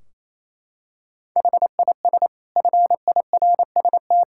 Morse code, heard as a keyed tone, about 28 wpm, 700 Hz. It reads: HIS FIRST
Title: Sentences from Top 1600 Words 28wpm